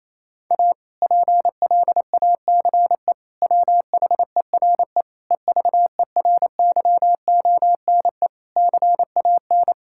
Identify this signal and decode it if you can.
Morse code, heard as a keyed tone, about 28 wpm, 700 Hz. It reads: A PLACE WHERE EVERYONE CAN
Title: Sentences from Top 1000 Words - Review Part 5 - 28wpm